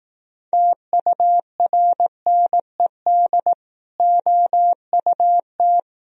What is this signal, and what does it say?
Morse code, heard as a keyed tone, about 18 wpm, 700 Hz. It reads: TURNED OUT
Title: Speed-Racing - Sets of 5 Words 15wpm